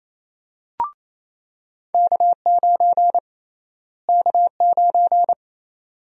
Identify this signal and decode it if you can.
Morse code, heard as a keyed tone, about 28 wpm, 700 Hz. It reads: K9 K9
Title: USA Call Sign Prefixes - Repeated 2x - 28wpm